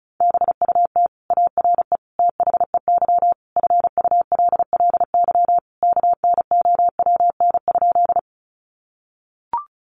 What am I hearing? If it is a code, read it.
Morse code, heard as a keyed tone, about 34 wpm, 700 Hz. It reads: BUT ARE THEY FULLY KNOWN?
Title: Sentences from Top 1300 Words 35wpm